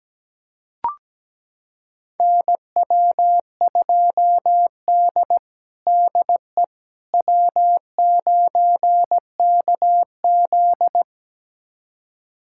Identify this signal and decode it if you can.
Morse code, heard as a keyed tone, about 17 wpm, 700 Hz. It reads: NW2D DE W9KZ
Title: USA Callsign DE Callsign - 4 Characters - 17wpm